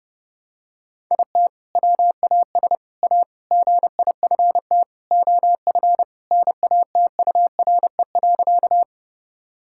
Morse code, heard as a keyed tone, about 30 wpm, 700 Hz. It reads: IT WAS A GIFT OF NATURE.
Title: Sentences from Top 1000 Words - Review Part 6 - 30wpm